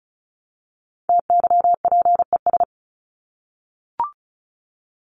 Morse code, heard as a keyed tone, about 34 wpm, 700 Hz. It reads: TYPES
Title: ES:S Suffix - Top 100 Words with Base Words 35wpm